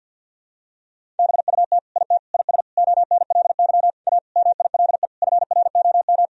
Morse code, heard as a keyed tone, about 52 wpm, 700 Hz. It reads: BUT IT IS ONLY A MILE FROM
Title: Sentences from Top 1000 Words - Review Part 1 - 50wpm